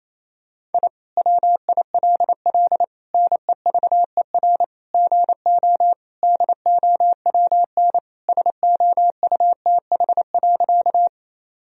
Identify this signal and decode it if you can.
Morse code, heard as a keyed tone, about 27 wpm, 700 Hz. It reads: I WILL NEVER GO DOWN SOUTH.
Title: Sentences from Top 1000 Words - Review Part 2 - 28wpm